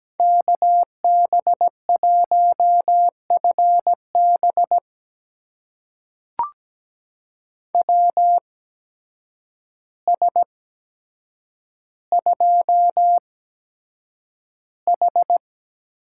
Morse code, heard as a keyed tone, about 17 wpm, spaced as for 8 wpm, 700 Hz. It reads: KB1FB WS2H
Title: USA Call Signs - 8x Character Spacing - 17wpm